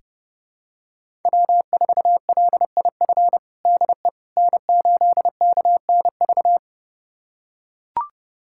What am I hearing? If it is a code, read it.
Morse code, heard as a keyed tone, about 30 wpm, 700 Hz. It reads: W4LIF DE N8KNV
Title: USA Callsign DE Callsign - 5 Characters - 30wpm